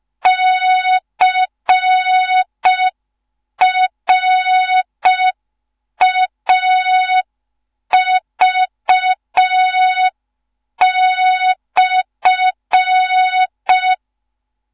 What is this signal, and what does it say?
Morse code, heard as a keyed tone, about 5 wpm, 3700 Hz. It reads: CRAV/
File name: Morze-HA2MN / hang / 15 GYAKORLAT